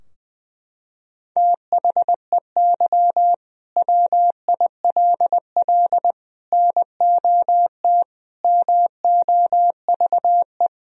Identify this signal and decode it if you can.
Morse code, heard as a keyed tone, about 21 wpm, 700 Hz. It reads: THEY WILL NOT MOVE
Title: Sentences from Top 1000 Words - Review Part 3 - 20wpm